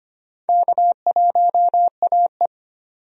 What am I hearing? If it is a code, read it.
Morse code, heard as a keyed tone, about 25 wpm, 700 Hz. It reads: K1AE
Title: USA Call Signs - 4 Characters - 25wpm